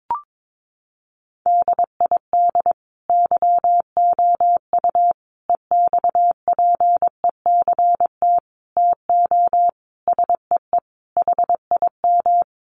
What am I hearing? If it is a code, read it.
Morse code, heard as a keyed tone, about 22 wpm, 700 Hz. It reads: DID YOU EXPECT TO SEE HIM
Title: Sentences from Top 1100 Words 22wpm